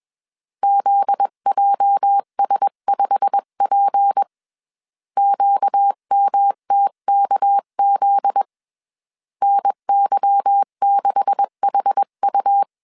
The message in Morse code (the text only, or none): ZJS5P QMTKZ NY6HU